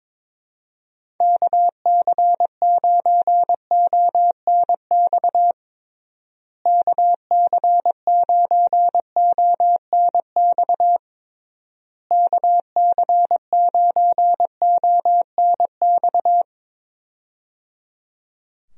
Morse code, heard as a keyed tone, about 22 wpm, 700 Hz. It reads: KC9ONX KC9ONX KC9ONX
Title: USA Call Signs - 6 Characters - Repeated 3x - 22wpm